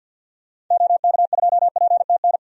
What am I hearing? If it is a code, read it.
Morse code, heard as a keyed tone, about 48 wpm, 700 Hz. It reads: OK2PTN
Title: DXCC-100 Call Signs - 6 Characters - 50wpm